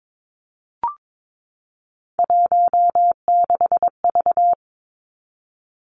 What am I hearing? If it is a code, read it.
Morse code, heard as a keyed tone, about 22 wpm, 700 Hz. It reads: 16V
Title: Three Challenging Letter-Number Combinations 22wpm